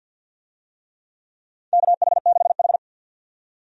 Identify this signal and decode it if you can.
Morse code, heard as a keyed tone, about 52 wpm, 700 Hz. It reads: KH6H
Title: USA Call Signs - 4 Characters - 50wpm